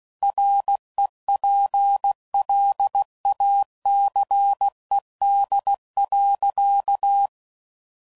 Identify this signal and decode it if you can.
Morse code, heard as a keyed tone, about 16 wpm, 800 Hz. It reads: REPLACED.
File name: Sayings / Chuckles / Squeaky